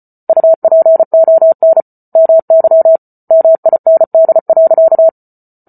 Morse code, heard as a keyed tone, about 34 wpm, 650 Hz. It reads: UPON MY MIND.